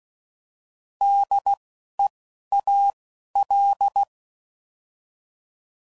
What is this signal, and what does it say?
Morse code, heard as a keyed tone, about 16 wpm, spaced as for 11 wpm, 800 Hz. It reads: DEAL
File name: Prefixes / By First Letter / Ds